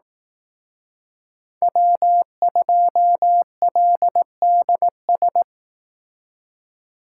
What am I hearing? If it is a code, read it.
Morse code, heard as a keyed tone, about 18 wpm, 700 Hz. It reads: W2LDS
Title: Speed Racing - USA 1x3 Call Signs 15wpm